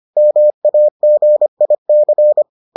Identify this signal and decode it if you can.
Morse code, heard as a keyed tone, about 25 wpm, 600 Hz. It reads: MAGIC